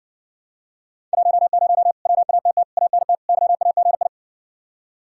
Morse code, heard as a keyed tone, about 60 wpm, 700 Hz. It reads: 10 WATT ANT YAGI